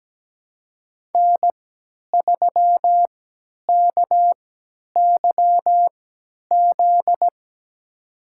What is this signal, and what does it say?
Morse code, heard as a keyed tone, about 17 wpm, spaced as for 10 wpm, 700 Hz. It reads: N3KYZ
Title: USA Call Signs - 5 Characters - 3x Character Spacing - 17wpm